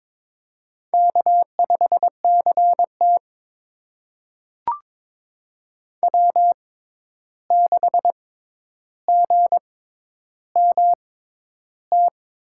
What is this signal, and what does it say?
Morse code, heard as a keyed tone, about 22 wpm, spaced as for 11 wpm, 700 Hz. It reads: K5CT W6GMT
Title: USA Call Signs - 6x Character Spacing - 22wpm